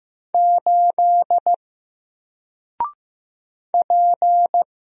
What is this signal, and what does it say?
Morse code, heard as a keyed tone, about 15 wpm, 700 Hz. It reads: 8 P
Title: Single Character (Letter:Number:Punctuation:Prosign) 15wpm